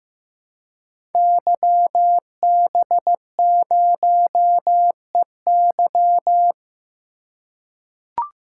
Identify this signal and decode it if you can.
Morse code, heard as a keyed tone, about 15 wpm, 700 Hz. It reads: YB0EY
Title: Code Groups - 5 Letter:Numbers - 15wpm